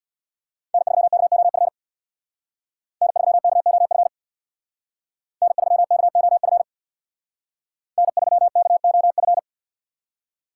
Morse code, heard as a keyed tone, about 63 wpm, 700 Hz. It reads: N3KOF N3KOF N3KOF N3KOF